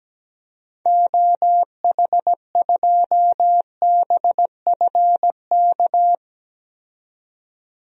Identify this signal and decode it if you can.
Morse code, heard as a keyed tone, about 17 wpm, 700 Hz. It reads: OH2BFK
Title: DXCC-100 Call Signs - 6 Characters - 17wpm